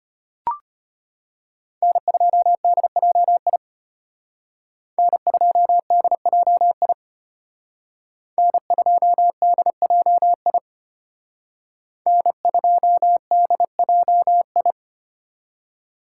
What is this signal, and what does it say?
Morse code, heard as a keyed tone, about 32 wpm, 700 Hz. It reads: N2DJI N2DJI N2DJI N2DJI